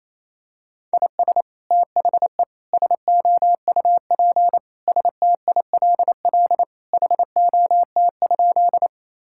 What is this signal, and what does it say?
Morse code, heard as a keyed tone, about 28 wpm, 700 Hz. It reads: IS THE SOUP STILL HOT?
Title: Sentences from Top 1000 Words - Review Part 6 - 28wpm